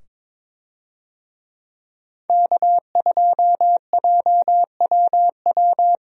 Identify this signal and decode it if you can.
Morse code, heard as a keyed tone, about 21 wpm, 700 Hz. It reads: K2JWW
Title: USA Call Signs - 5 Characters - 22wpm